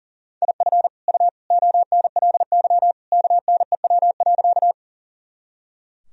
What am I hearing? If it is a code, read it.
Morse code, heard as a keyed tone, about 37 wpm, 700 Hz. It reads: IF U ONLY KNEW.
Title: Sentences from Top 400 Words and Abbreviations 40wpm